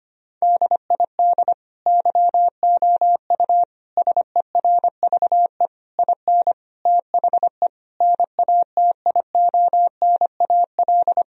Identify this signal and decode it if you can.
Morse code, heard as a keyed tone, about 25 wpm, 700 Hz. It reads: DID YOU SERVE IN THE NATIONAL
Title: Sentences from Top 1000 Words - Review Part 4 - 25wpm